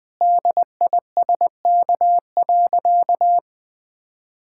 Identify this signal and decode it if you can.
Morse code, heard as a keyed tone, about 20 wpm, 700 Hz. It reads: DISK.
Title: Sentences from Top 1000 Words - Review Part 6 - 20wpm